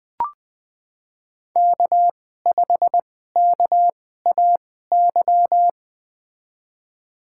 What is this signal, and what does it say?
Morse code, heard as a keyed tone, about 20 wpm, spaced as for 15 wpm, 700 Hz. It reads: K5KAY
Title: USA Call Signs - 5 Characters - 2x Character Spacing - 20wpm